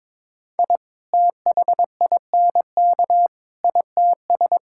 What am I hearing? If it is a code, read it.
Morse code, heard as a keyed tone, about 22 wpm, 700 Hz. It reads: I THINK ITS
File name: practice / 22wpm / Sets of 7 Words - 22wpm